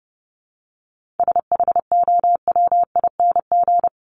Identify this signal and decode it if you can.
Morse code, heard as a keyed tone, about 30 wpm, 700 Hz. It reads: SHOWING
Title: ING Suffix - Top 100 Words with Base Words 30wpm